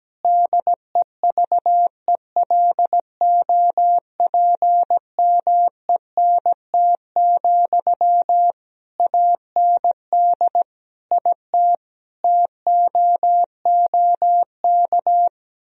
Morse code, heard as a keyed tone, about 17 wpm, 700 Hz. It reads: DEVELOPMENT, AND IT TOOK
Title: Sentences from Top 1000 Words - Review Part 4 - 17wpm